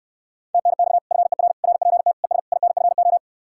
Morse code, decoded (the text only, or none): TNX FER RPT ES INFO